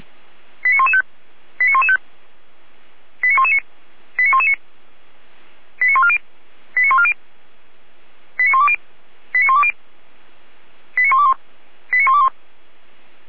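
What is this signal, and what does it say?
Five-tone selective call (ZVEI), heard as quick runs of five tones, tones of about 70 ms. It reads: 89115 89115 89119 89119 89140 89140 89120 89120 89121 89121
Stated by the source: ZVEI five-tone call received on an SDR, fire brigade Orlamünde.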